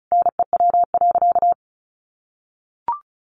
Morse code, heard as a keyed tone, about 34 wpm, 700 Hz. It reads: NEW.
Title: Sentences from Top 1100 Words 35wpm